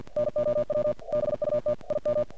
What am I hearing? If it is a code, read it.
Morse code, heard as a keyed tone, about 48 wpm, 590 Hz, with noise. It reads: NOW BUT IM
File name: archived / CRSnet